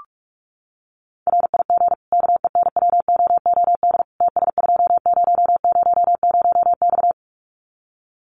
Morse code, heard as a keyed tone, about 44 wpm, 700 Hz. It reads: RIG KENWOOD TS2000X
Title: Common QSO Phrases 45wpm